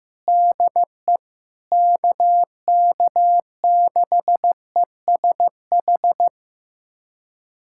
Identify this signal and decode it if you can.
Morse code, heard as a keyed tone, about 15 wpm, 700 Hz. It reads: DE KK6ESH
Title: USA Callsign DE Callsign 15wpm